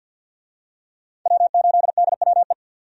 Morse code, heard as a keyed tone, about 50 wpm, 700 Hz. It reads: W8DWE
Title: USA Call Signs - 5 Characters - 50wpm